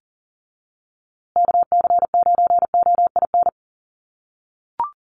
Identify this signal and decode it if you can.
Morse code, heard as a keyed tone, about 40 wpm, 700 Hz. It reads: KC9OIN
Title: USA Call Signs Repeated 2x 40wpm